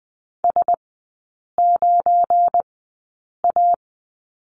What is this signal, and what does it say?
Morse code, heard as a keyed tone, about 20 wpm, spaced as for 8 wpm, 700 Hz. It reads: S9A
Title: 184 Lesson 30e - Letter b - Three Characters 20wpm